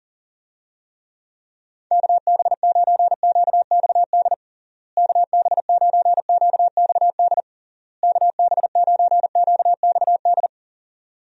Morse code, heard as a keyed tone, about 40 wpm, 700 Hz. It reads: KB9QXD KB9QXD KB9QXD
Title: USA Call Signs - 6 Characters - Repeated 3x - 40wpm